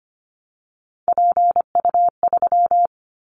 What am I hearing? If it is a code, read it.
Morse code, heard as a keyed tone, about 25 wpm, 700 Hz. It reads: PU3
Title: Callsign Prefixes for Top 100 DXCC - 25wpm